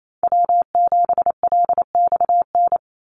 Morse code, heard as a keyed tone, about 28 wpm, 700 Hz. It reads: W7LXN
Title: Call Signs Repeated 3x - 28wpm